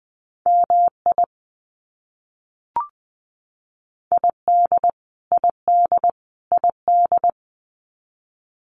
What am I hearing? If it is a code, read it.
Morse code, heard as a keyed tone, about 20 wpm, 700 Hz. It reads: MI ID ID ID
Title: US States Repeated 3x 20wpm